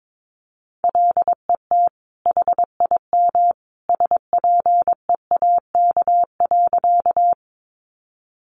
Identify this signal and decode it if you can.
Morse code, heard as a keyed tone, about 22 wpm, 700 Hz. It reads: LET HIM SPEAK.
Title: Sentences from Top 1100 Words 22wpm